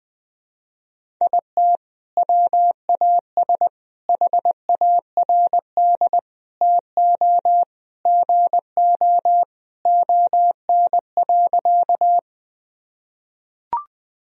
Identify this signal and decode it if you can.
Morse code, heard as a keyed tone, about 20 wpm, 700 Hz. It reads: IT WAS HARD TO GO ON.
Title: Sentences from Top 1000 Words - Review Part 1 - 20wpm